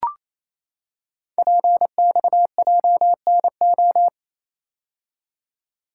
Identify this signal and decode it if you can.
Morse code, heard as a keyed tone, about 28 wpm, 700 Hz. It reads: PXJNO
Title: Code Groups - 5 Letters - 28wpm